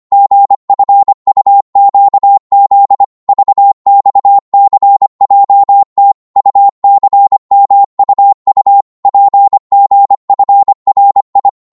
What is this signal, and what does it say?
Morse code, heard as a keyed tone, about 25 wpm, 800 Hz. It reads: GFUQZ VXCJT UCMUU PGFRI
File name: Morse / ltr-25wpm